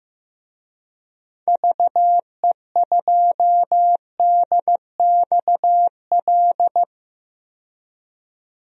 Morse code, heard as a keyed tone, about 15 wpm, 700 Hz. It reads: VE2DXL
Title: DXCC-100 Call Signs - 6 Characters - 15wpm